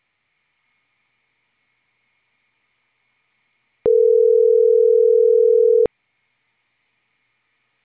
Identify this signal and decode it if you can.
Ringback tone (North America), 440 + 480 Hz.